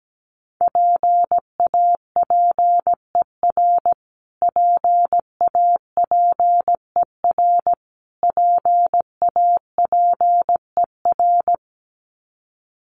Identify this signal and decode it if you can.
Morse code, heard as a keyed tone, about 17 wpm, 700 Hz. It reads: PAPER PAPER PAPER
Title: Top 500 Words Repeated 17wpm